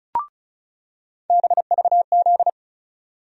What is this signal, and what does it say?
Morse code, heard as a keyed tone, about 34 wpm, 700 Hz. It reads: BVZ
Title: USA Call Sign Postfixes - 3 Letters 35wpm